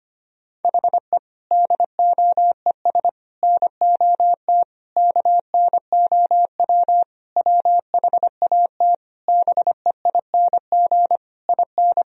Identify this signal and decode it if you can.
Morse code, heard as a keyed tone, about 25 wpm, 700 Hz. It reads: HE DOES NOT KNOW WHAT BEING IN
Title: Sentences from Top 1000 Words - Review Part 3 - 25wpm